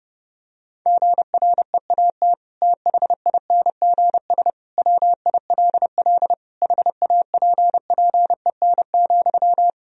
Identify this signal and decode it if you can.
Morse code, heard as a keyed tone, about 30 wpm, 700 Hz. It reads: GREAT THINGS WILL HAPPEN,
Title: Sentences from Top 1000 Words - Review Part 4 - 30wpm